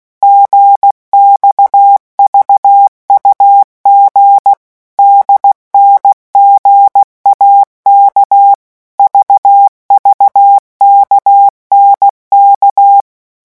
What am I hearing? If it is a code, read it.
Morse code, heard as a keyed tone, about 16 wpm, 800 Hz. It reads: GXVUG DNGAK VVKNK
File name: morse mp3 / revision3-vitesse 15 mots